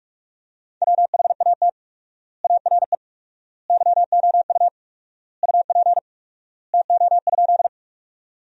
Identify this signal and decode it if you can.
Morse code, heard as a keyed tone, about 42 wpm, 700 Hz. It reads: WHAT ARE YOU UP TO?
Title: 4x Word-Spacing - Sentences from Top 100 Words - 45wpm